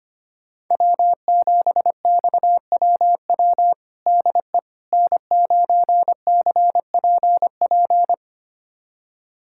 Morse code, heard as a keyed tone, about 25 wpm, 700 Hz. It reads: W7XWW DE N9CPP
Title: USA Callsign DE Callsign - 5 Characters - 25wpm